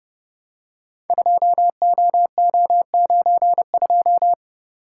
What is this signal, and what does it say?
Morse code, heard as a keyed tone, about 30 wpm, 700 Hz. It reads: 2OO92
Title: Code Groups - 5 Letter:Numbers - 30wpm